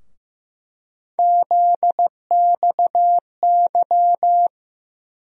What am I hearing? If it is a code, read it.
Morse code, heard as a keyed tone, about 15 wpm, 700 Hz. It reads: ZXY
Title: USA Call Sign Postfixes - 3 Letters 15wpm